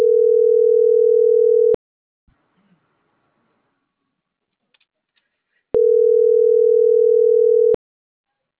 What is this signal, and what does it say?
Ringback tone (North America), 440 + 480 Hz.